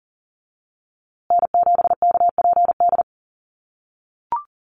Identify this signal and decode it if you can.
Morse code, heard as a keyed tone, about 40 wpm, 700 Hz. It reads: N7KPD
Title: USA Call Signs Repeated 3x 40wpm